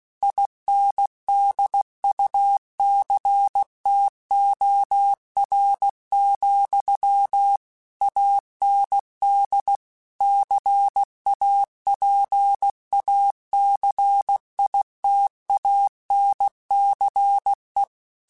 Morse code, heard as a keyed tone, about 16 wpm, 800 Hz. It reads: INDUCTOR, AND CAPACITANCE
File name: Technical Subjects / Technical Subjects-mp3s / Reactance @15